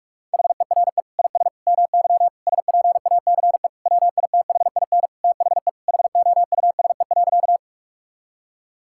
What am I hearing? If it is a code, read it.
Morse code, heard as a keyed tone, about 46 wpm, 700 Hz. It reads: HERE IS MY SPACE WITHIN THE HOUSE.
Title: Sentences from Top 1000 Words - Review Part 5 - 45wpm